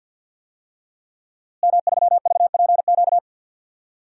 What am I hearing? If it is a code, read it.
Morse code, heard as a keyed tone, about 53 wpm, 700 Hz. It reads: M3VPQ
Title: DXCC-100 Call Signs - 5 Characters - Repeated 3x - 50wpm